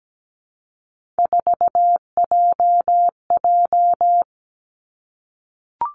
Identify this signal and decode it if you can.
Morse code, heard as a keyed tone, about 17 wpm, 700 Hz. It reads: 4JJ
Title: Three Challenging Letter-Number Combinations 17wpm